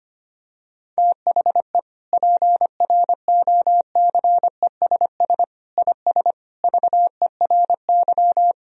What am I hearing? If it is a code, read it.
Morse code, heard as a keyed tone, about 25 wpm, 700 Hz. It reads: THE PROCESS IS VERY
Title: Sentences from Top 1000 Words - Review Part 3 - 25wpm